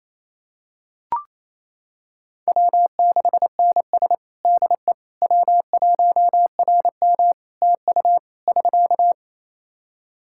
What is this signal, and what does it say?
Morse code, heard as a keyed tone, about 27 wpm, 700 Hz. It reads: W6NS DE W1RM TU <SK>
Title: Practice QSO - Exchange 4 - Ending - 28wpm